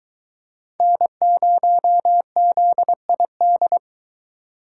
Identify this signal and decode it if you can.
Morse code, heard as a keyed tone, about 24 wpm, 700 Hz. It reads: N0ZID